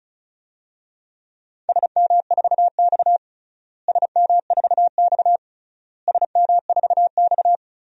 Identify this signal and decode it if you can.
Morse code, heard as a keyed tone, about 34 wpm, 700 Hz. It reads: SM4X SM4X SM4X
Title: DXCC-100 Call Signs - 4 Characters - Repeated 3x - 35wpm